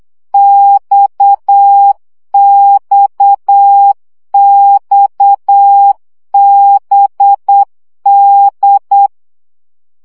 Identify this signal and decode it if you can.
Morse code, heard as a keyed tone, about 8 wpm, 800 Hz. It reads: XXXBD
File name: CW gyakorlas d b x 7 baud